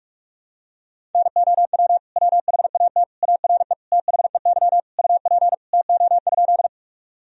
Morse code, heard as a keyed tone, about 46 wpm, 700 Hz. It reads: NOW WHAT ARE THEY UP TO?
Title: Sentences from Top 1000 Words - Review Part 1 - 45wpm